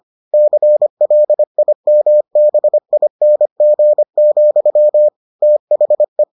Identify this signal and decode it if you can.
Morse code, heard as a keyed tone, about 25 wpm, 600 Hz. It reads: CLIMBING, THE